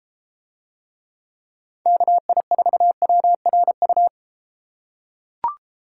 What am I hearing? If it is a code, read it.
Morse code, heard as a keyed tone, about 31 wpm, 700 Hz. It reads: KI4WRU